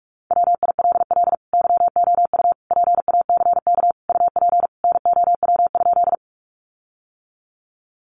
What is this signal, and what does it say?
Morse code, heard as a keyed tone, about 44 wpm, 700 Hz. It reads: WILL YOU PACK UP NOW?
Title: Sentences from Top 1300 Words 45wpm